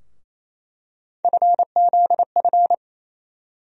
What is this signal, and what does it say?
Morse code, heard as a keyed tone, about 28 wpm, 700 Hz. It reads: FZF
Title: USA Call Sign Postfixes - 3 Letters 28wpm